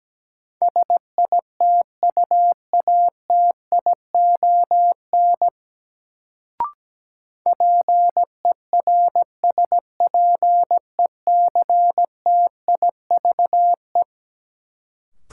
Morse code, heard as a keyed tone, about 17 wpm, 700 Hz. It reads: SITUATION PERSPECTIVE
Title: Sentences from Top 1200 Words - New Word Drill Alt - 17wpm